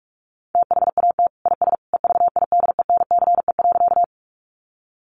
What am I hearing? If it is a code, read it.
Morse code, heard as a keyed tone, about 44 wpm, 700 Hz. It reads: THAT IS EVIDENCE.
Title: Sentences from Top 1300 Words 45wpm